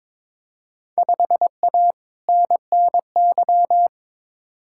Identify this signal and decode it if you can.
Morse code, heard as a keyed tone, about 22 wpm, 700 Hz. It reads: 5A NNY
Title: Field Day Exchanges - 22wpm